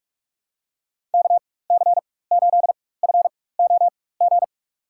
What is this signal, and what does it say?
Morse code, heard as a keyed tone, about 44 wpm, spaced as for 21 wpm, 700 Hz. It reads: KC8FOG
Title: USA Call Signs - 6 Characters - 4x Character Spacing - 45wpm